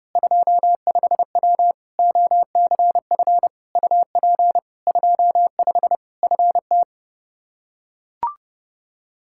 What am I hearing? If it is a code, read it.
Morse code, heard as a keyed tone, about 30 wpm, 700 Hz. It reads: 25W OCF UP 25 FT
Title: Practice QSO - Exchange 2 - Info - 30wpm